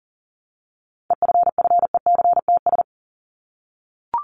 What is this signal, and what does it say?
Morse code, heard as a keyed tone, about 40 wpm, 700 Hz. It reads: EFFECTS
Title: ES:S Suffix - Top 100 Words with Base Words 40wpm